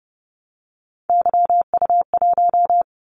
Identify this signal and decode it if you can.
Morse code, heard as a keyed tone, about 32 wpm, 700 Hz. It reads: YU1
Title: Callsign Prefixes for Top 25 DXCC - 30wpm